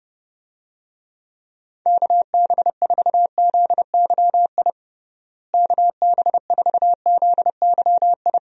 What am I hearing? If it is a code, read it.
Morse code, heard as a keyed tone, about 32 wpm, 700 Hz. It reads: KB4ZYI KB4ZYI
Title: USA Call Signs - 6 Characters - Repeated 2x - 30wpm